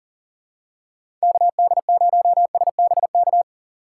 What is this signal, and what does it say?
Morse code, heard as a keyed tone, about 37 wpm, 700 Hz. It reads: KD0SBK
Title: USA Call Signs - 2x Character Spacing - 40wpm